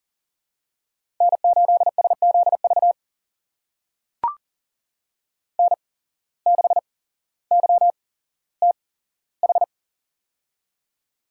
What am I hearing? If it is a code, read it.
Morse code, heard as a keyed tone, about 37 wpm, spaced as for 17 wpm, 700 Hz. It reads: N8SZV N6YTH
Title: USA Call Signs - 5 Characters - 8x Character Spacing - 40wpm